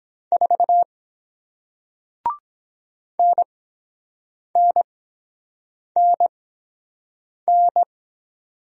Morse code, heard as a keyed tone, about 23 wpm, spaced as for 7 wpm, 700 Hz. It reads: 4 NNNN